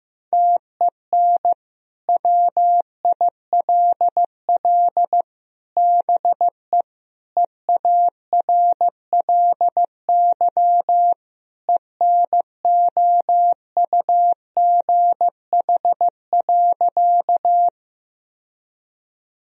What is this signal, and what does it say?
Morse code, heard as a keyed tone, about 15 wpm, 700 Hz. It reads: TEN WILL BE EARLY ENOUGH.
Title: Sentences from Top 1000 Words - Review Part 2 - 15wpm